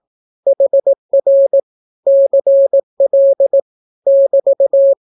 Morse code, heard as a keyed tone, about 18 wpm, 550 Hz. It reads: HR CL <BT>
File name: naukapodstaw / skroty / skroty